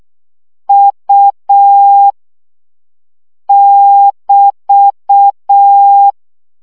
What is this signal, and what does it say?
Morse code, heard as a keyed tone, about 6 wpm, 800 Hz. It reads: U <BT>
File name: CW gyakorlas u 5 baud